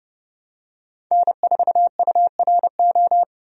Morse code, heard as a keyed tone, about 29 wpm, 700 Hz. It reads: N4URO
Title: USA Call Signs - 5 Characters - 30wpm